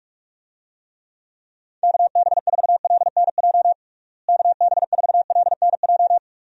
Morse code, heard as a keyed tone, about 46 wpm, 700 Hz. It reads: KB4LNJ KB4LNJ
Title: USA Call Signs - 6 Characters - Repeated 3x - 45wpm